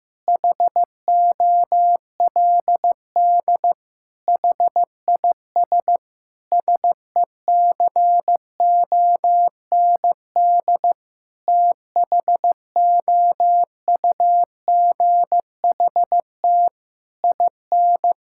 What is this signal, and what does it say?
Morse code, heard as a keyed tone, about 16 wpm, 700 Hz. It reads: HOLD HIS SECOND THOUGHT IN
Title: Sentences from Top 1000 Words - Review Part 6 - 15wpm